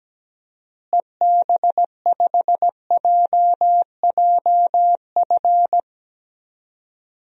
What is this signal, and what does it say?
Morse code, heard as a keyed tone, about 17 wpm, 700 Hz. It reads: EB5JJF
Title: DXCC-100 Call Signs - 6 Characters - Repeated 2x - 17wpm